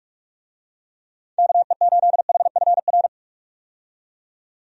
Morse code, heard as a keyed tone, about 46 wpm, 700 Hz. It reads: KE8HFR
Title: USA Call Signs - 6 Characters - 45wpm